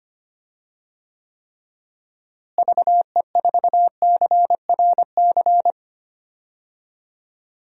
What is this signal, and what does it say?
Morse code, heard as a keyed tone, about 25 wpm, 700 Hz. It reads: VE4CRC
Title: DXCC-100 Call Signs - 6 Characters - Repeated 2x - 25wpm